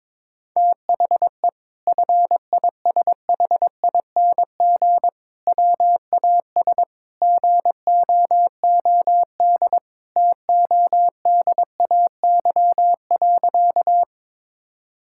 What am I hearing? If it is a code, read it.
Morse code, heard as a keyed tone, about 21 wpm, 700 Hz. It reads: THE FISHING WAS GOOD TODAY.
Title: Sentences from Top 1000 Words - Review Part 5 - 22wpm